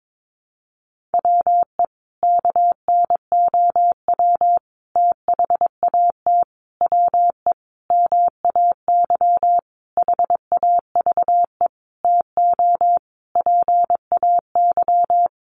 Morse code, heard as a keyed tone, about 22 wpm, 700 Hz. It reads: WE KNOW THAT WE MAY HAVE TO PAY
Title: Sentences from Top 1100 Words 22wpm